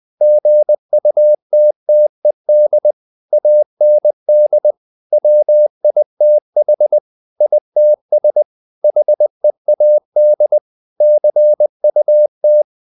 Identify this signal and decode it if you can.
Morse code, heard as a keyed tone, about 20 wpm, 600 Hz. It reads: GUTTED AND WITH ITS HEAD CUT
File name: the old man and the sea part 2 20wpm